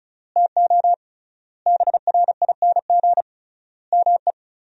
Morse code, heard as a keyed tone, about 34 wpm, 700 Hz. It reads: TO BRING ME
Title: 3x Word-Spacing - Sentences from Top 500 Words - 35wpm